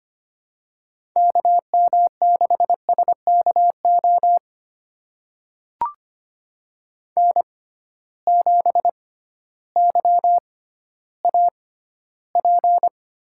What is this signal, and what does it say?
Morse code, heard as a keyed tone, about 25 wpm, spaced as for 14 wpm, 700 Hz. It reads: KM6SKO N7YAP
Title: USA Call Signs - 6x Character Spacing - 25wpm